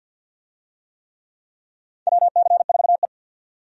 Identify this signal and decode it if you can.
Morse code, heard as a keyed tone, about 48 wpm, 700 Hz. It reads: WC4E
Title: USA Call Signs - 4 Characters - 50wpm